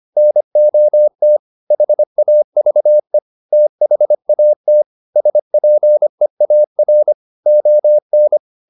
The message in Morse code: NOT HAVE THAT SPEAR ON

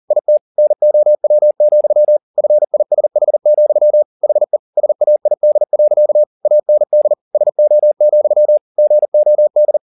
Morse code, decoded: IT NOW, FISH, HE SAID. AND SO, GOD

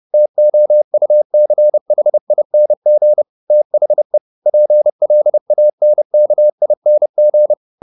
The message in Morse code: TOUCHING THE PLANKING

